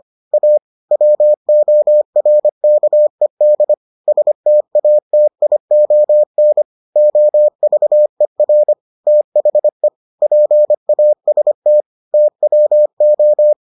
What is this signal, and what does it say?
Morse code, heard as a keyed tone, about 25 wpm, 600 Hz. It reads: A WORKED STATION OVER THE PAST TWO